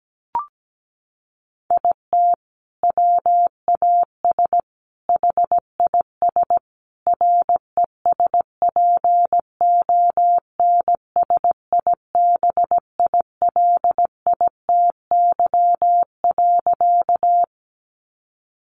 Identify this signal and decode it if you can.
Morse code, heard as a keyed tone, about 17 wpm, 700 Hz. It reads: IT WAS HIS RESPONSIBILITY.
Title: Sentences from Top 1200 Words 17wpm